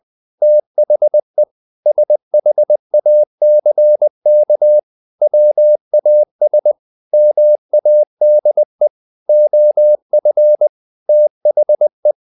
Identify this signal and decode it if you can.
Morse code, heard as a keyed tone, about 20 wpm, 600 Hz. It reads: THE SHACK WAS MADE OF THE
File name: the old man and the sea / the old man and the sea part 1 20wpm